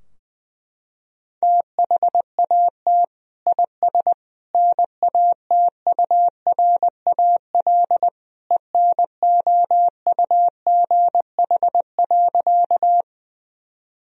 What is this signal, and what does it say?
Morse code, heard as a keyed tone, about 20 wpm, 700 Hz. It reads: THAT IS NATURAL ENOUGH.
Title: Sentences from Top 1000 Words - Review Part 4 - 20wpm